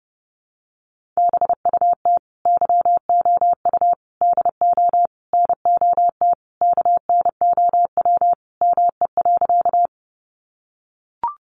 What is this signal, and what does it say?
Morse code, heard as a keyed tone, about 30 wpm, 700 Hz. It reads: BUT YOU DO NOT KNOW ME.
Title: Sentences from Top 600 Words 30wpm